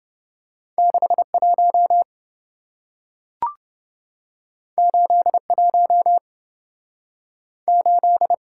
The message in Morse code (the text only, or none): 61 81 8